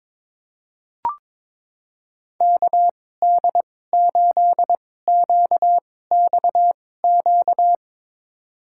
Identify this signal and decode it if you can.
Morse code, heard as a keyed tone, about 22 wpm, spaced as for 16 wpm, 700 Hz. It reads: KD8QXQ
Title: USA Call Signs - 6 Characters - 2x Character Spacing - 22wpm